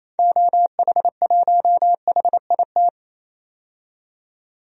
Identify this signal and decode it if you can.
Morse code, heard as a keyed tone, about 28 wpm, 700 Hz. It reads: OH1HIT
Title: DXCC-100 Call Signs - 6 Characters - 28wpm